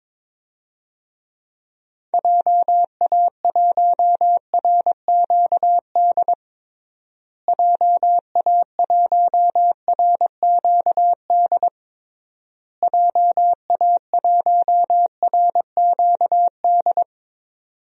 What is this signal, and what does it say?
Morse code, heard as a keyed tone, about 22 wpm, 700 Hz. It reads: JA1RQD JA1RQD JA1RQD
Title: DXCC-100 Call Signs - 6 Characters - Repeated 3x - 22wpm